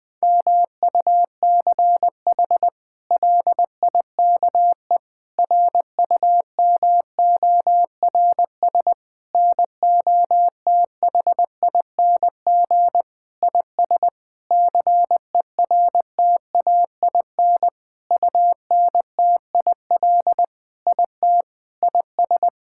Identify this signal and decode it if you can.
Morse code, heard as a keyed tone, about 20 wpm, 700 Hz. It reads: MUCH LIKE RUMORS NOTHING IS CERTAIN UNTIL IT IS
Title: Sentences with 8-Word Phrases - 20wpm